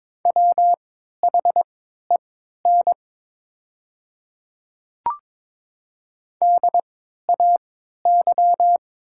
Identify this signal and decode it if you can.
Morse code, heard as a keyed tone, about 22 wpm, spaced as for 12 wpm, 690 Hz. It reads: WHEN DAY
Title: 3x Character Spacing - Top 200 Words - 22wpm